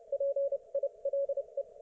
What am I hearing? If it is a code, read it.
Morse code, heard as a keyed tone, about 29 wpm, spaced as for 25 wpm, 560 Hz. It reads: PILE